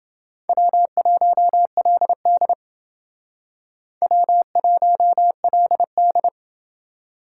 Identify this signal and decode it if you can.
Morse code, heard as a keyed tone, about 28 wpm, 700 Hz. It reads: W1LD W1LD